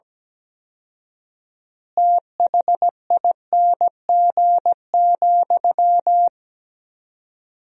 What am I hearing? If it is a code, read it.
Morse code, heard as a keyed tone, about 17 wpm, 700 Hz. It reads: THING,
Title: 4x Word-Spacing - Sentences from Top 200 Words - 17wpm